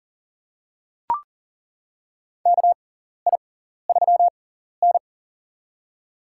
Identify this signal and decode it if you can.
Morse code, heard as a keyed tone, about 40 wpm, spaced as for 14 wpm, 700 Hz. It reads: KI3N
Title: USA Call Signs - 4 Characters - 6x Character Spacing - 40wpm